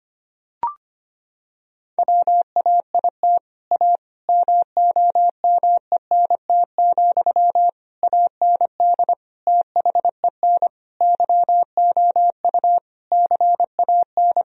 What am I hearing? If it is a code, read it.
Morse code, heard as a keyed tone, about 25 wpm, 700 Hz. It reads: WAIT A MOMENT, AND THEN YOU CAN
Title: Sentences from Top 1000 Words - Review Part 4 - 25wpm